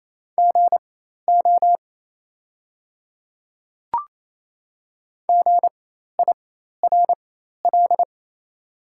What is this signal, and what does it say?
Morse code, heard as a keyed tone, about 28 wpm, spaced as for 13 wpm, 700 Hz. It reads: GO GIRL
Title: 4x Character Spacing - Top 500 Words - 28wpm